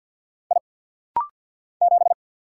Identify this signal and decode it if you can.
Morse code, heard as a keyed tone, about 48 wpm, 700 Hz. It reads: I 7
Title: Single Letter-Number - Rapid-Fire 50wpm